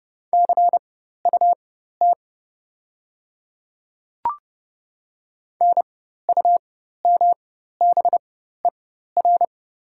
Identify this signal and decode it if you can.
Morse code, heard as a keyed tone, about 30 wpm, spaced as for 14 wpm, 700 Hz. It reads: CUT NUMBER
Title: 4x Character Spacing - Top 300 Words - 30wpm